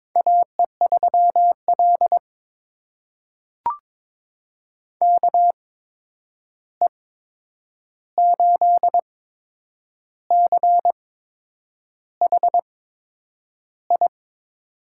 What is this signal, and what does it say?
Morse code, heard as a keyed tone, about 22 wpm, spaced as for 8 wpm, 700 Hz. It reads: AE3L KE8CHI
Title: USA Call Signs - 8x Character Spacing - 22wpm